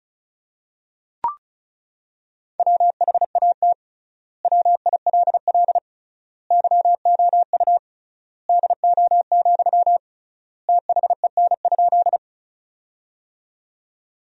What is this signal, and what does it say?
Morse code, heard as a keyed tone, about 36 wpm, 700 Hz. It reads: WHAT WILL YOU DO, THEN?
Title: 3x Word-Spacing - Sentences from Top 100 Words - 35wpm